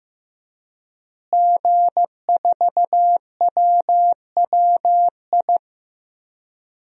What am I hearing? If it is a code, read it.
Morse code, heard as a keyed tone, about 16 wpm, 700 Hz. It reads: G4WWI
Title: DXCC-100 Call Signs - 5 Characters - Repeated 3x - 15wpm